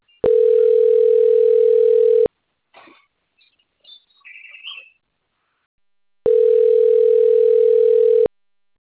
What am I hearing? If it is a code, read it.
Ringback tone (North America), 440 + 480 Hz.
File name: out- - - - - .84186